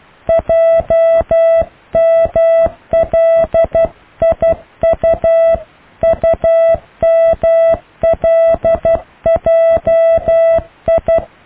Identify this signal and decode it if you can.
Morse code, heard as a keyed tone, about 12 wpm, 660 Hz. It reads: JMLIU UMLJI
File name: cw / IJLMU